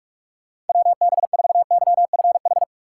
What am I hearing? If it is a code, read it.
Morse code, heard as a keyed tone, about 45 wpm, 700 Hz. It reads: WB4YFH
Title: USA Call Signs - 6 Characters - 4x Character Spacing - 45wpm